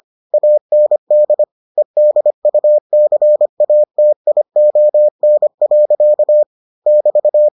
Morse code, heard as a keyed tone, about 25 wpm, 600 Hz. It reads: AND EDUCATION. <BT>